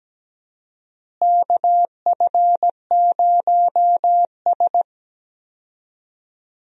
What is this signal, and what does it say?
Morse code, heard as a keyed tone, about 17 wpm, 700 Hz. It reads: KF0S
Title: USA Call Signs - 4 Characters - Repeated 3x - 17wpm